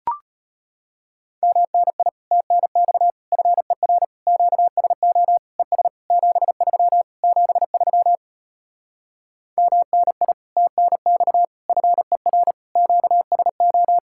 Morse code, heard as a keyed tone, about 33 wpm, 700 Hz. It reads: MNI TNX FER QSO ES 73 73 MNI TNX FER QSO